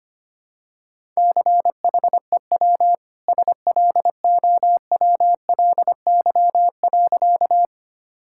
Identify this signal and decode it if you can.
Morse code, heard as a keyed tone, about 25 wpm, 700 Hz. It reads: CHEW SLOWLY.
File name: Easy Sentences with Hints - 25wpm